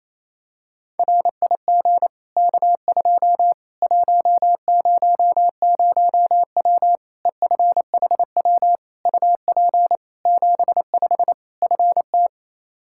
Morse code, heard as a keyed tone, about 28 wpm, 700 Hz. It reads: RIG K2 100W EFHW UP 75 FT
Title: Practice QSO - Exchange 2 - Info - 28wpm